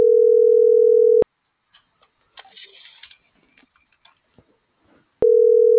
Ringback tone (North America), 440 + 480 Hz.